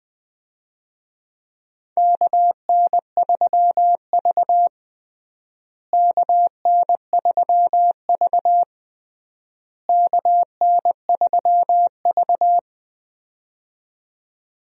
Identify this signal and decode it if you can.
Morse code, heard as a keyed tone, about 21 wpm, 700 Hz. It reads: KN3V KN3V KN3V
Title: USA Call Signs - 4 Characters - Repeated 3x - 20wpm